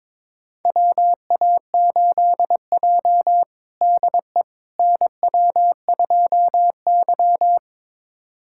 Morse code, heard as a keyed tone, about 22 wpm, 700 Hz. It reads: WA8J DE NW2Y
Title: USA Callsign DE Callsign - 4 Characters - 22wpm